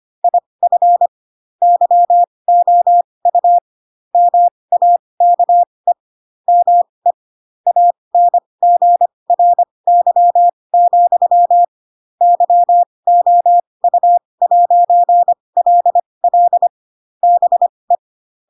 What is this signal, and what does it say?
Morse code, heard as a keyed tone, about 25 wpm, spaced as for 20 wpm, 700 Hz. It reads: IF YOU MAKE ME ANGRY, YOU'LL BE
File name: Pinocchio / Pinocchio ch4 15 25 ik1hge